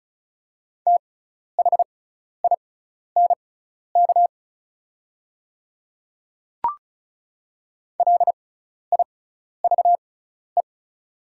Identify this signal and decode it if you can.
Morse code, heard as a keyed tone, about 36 wpm, spaced as for 12 wpm, 700 Hz. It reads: THINK LIVE
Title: 6x Character Spacing - Top 300 Words - 35wpm